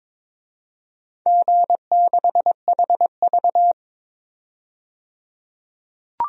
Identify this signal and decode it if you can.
Morse code, heard as a keyed tone, about 22 wpm, 700 Hz. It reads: G6HV
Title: DXCC-100 Call Signs - 4 Characters - 22wpm